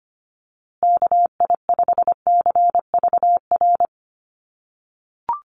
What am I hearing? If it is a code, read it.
Morse code, heard as a keyed tone, about 25 wpm, 700 Hz. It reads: KI5CVR
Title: USA Call Signs Repeated 2x 25wpm